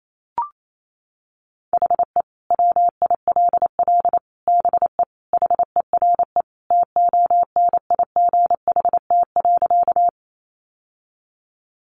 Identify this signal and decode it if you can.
Morse code, heard as a keyed tone, about 28 wpm, 700 Hz. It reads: HE WILL BE HERE TONIGHT.
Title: Sentences from Top 600 Words 28wpm